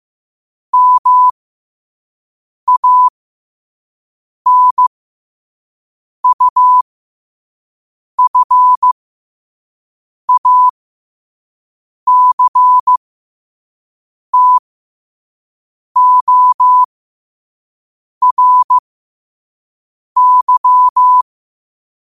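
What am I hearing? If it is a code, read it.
Morse code, heard as a keyed tone, about 15 wpm, spaced as for 5 wpm, 1000 Hz. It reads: MANUFACTORY
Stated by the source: Morse code quote of the day at 5 wpm.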